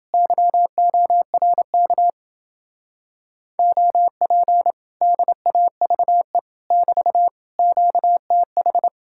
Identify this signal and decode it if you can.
Morse code, heard as a keyed tone, about 29 wpm, 700 Hz. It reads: YORK OP DAVE <BT> QTH